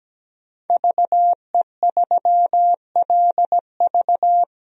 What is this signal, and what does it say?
Morse code, heard as a keyed tone, about 17 wpm, 700 Hz. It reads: VE3LV
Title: DXCC-100 Call Signs - 5 Characters - 17wpm